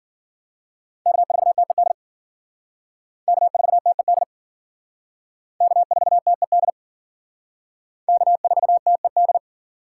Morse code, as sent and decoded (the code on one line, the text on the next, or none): -.- ....- - . -.. / -.- ....- - . -.. / -.- ....- - . -.. / -.- ....- - . -..
K4TED K4TED K4TED K4TED